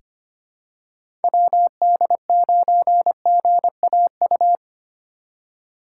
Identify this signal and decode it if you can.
Morse code, heard as a keyed tone, about 25 wpm, 700 Hz. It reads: WD9GAU
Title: USA Call Signs - 6 Characters - 6x Character Spacing - 25wpm